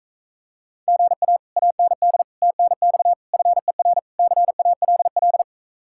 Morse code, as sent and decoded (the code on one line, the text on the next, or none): --. .- / .- -. -.. / - -. -..- / ..-. .-. / -.-. .- .-.. .-..
GA AND TNX FER CALL